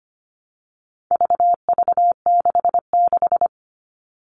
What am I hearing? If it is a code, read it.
Morse code, heard as a keyed tone, about 25 wpm, 700 Hz. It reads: VV66
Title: All Cut Numbers - 4-Digit 25wpm